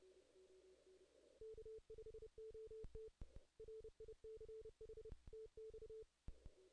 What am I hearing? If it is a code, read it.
Morse code, heard as a keyed tone, about 30 wpm, 440 Hz. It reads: K5OT RICH TX
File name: 2016 JAN CW NAQP / 2016 Jan naqp k5ot - 0-9